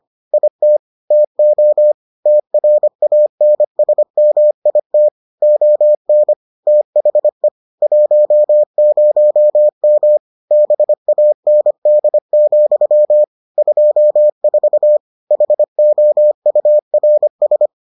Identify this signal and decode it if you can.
Morse code, heard as a keyed tone, about 25 wpm, 600 Hz. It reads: IT TO TRANSMIT ON THE 10M BAND, 24 HOURS